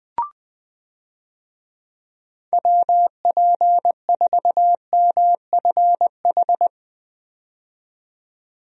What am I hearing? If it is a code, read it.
Morse code, heard as a keyed tone, about 20 wpm, 700 Hz. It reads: WP4MFH
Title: USA Call Signs - 6 Characters - 20wpm